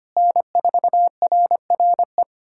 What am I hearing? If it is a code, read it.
Morse code, heard as a keyed tone, about 25 wpm, 700 Hz. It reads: N4RRE
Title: USA Callsign DE Callsign - 5 Characters - 25wpm